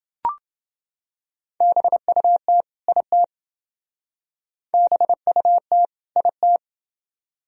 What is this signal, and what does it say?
Morse code, heard as a keyed tone, about 28 wpm, 700 Hz. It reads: BUT IT BUT IT